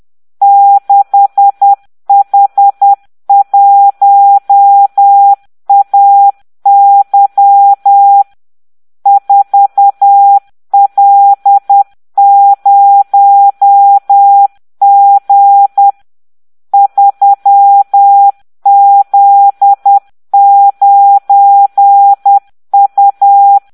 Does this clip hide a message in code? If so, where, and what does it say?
Yes: Morse code from 0.4 to 23.6 s, heard as a keyed tone, about 10 wpm, 800 Hz. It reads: 6H1AY 4L0G 3Z9U